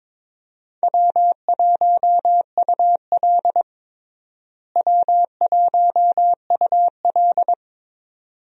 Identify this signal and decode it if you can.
Morse code, heard as a keyed tone, about 22 wpm, 700 Hz. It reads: W1UL W1UL
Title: USA Call Signs - 4 Characters - Repeated 2x - 22wpm